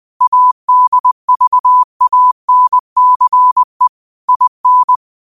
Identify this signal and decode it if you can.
Morse code, heard as a keyed tone, about 20 wpm, 1000 Hz. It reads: ADVANCE IN